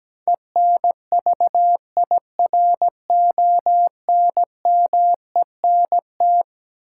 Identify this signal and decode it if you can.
Morse code, heard as a keyed tone, about 17 wpm, 700 Hz. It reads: ENVIRONMENT
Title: Sentences from Top 1000 Words - Review Part 5 - New Word Drill - 17wpm